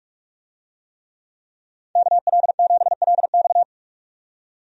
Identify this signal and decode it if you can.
Morse code, heard as a keyed tone, about 45 wpm, 700 Hz. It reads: KL7LX
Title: DXCC-100 Call Signs - 5 Characters - 45wpm